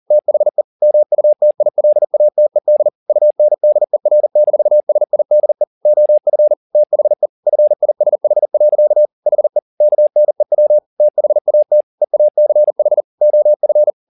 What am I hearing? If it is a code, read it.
Morse code, heard as a keyed tone, about 40 wpm, 600 Hz. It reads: THE MUTILATED UNDER-SIDE OF THE FISH. HE KNEW THAT EACH OF
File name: the old man and the sea 2 40wpm